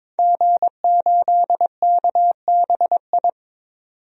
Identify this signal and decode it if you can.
Morse code, heard as a keyed tone, about 22 wpm, 700 Hz. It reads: G8KBI
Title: DXCC-100 Call Signs - 5 Characters - 22wpm